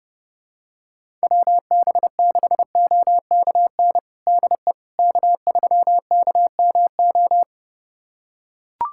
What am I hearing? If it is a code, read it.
Morse code, heard as a keyed tone, about 32 wpm, 700 Hz. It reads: WB6OKN DE K3KMO
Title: USA Callsign DE Callsign 30wpm